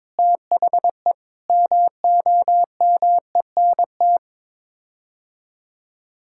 Morse code, heard as a keyed tone, about 21 wpm, 700 Hz. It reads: THE MOMENT
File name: practice / 22wpm / Sets of 6 Words - 22wpm